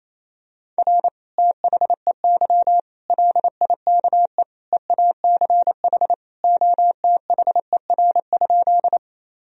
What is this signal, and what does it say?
Morse code, heard as a keyed tone, about 28 wpm, 700 Hz. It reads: R THEY LIKE EACH OTHER?
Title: Sentences from Top 200 Words and Abbreviations 28wpm